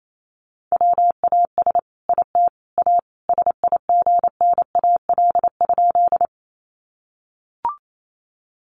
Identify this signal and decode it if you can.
Morse code, heard as a keyed tone, about 28 wpm, 700 Hz. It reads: WAS IT A SIGNAL?
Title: Sentences from Top 1400 Words 28wpm